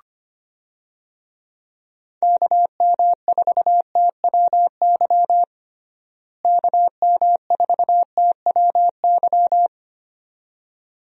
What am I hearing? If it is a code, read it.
Morse code, heard as a keyed tone, about 25 wpm, 700 Hz. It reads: KM4TWY KM4TWY
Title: USA Call Signs - 6 Characters - Repeated 2x - 25wpm